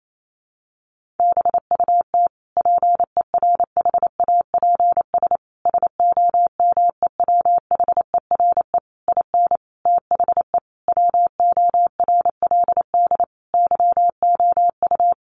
Morse code, heard as a keyed tone, about 28 wpm, 700 Hz. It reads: BUT PERHAPS SOMEWHERE IN THE WORLD YOU
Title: Sentences from Top 1400 Words 28wpm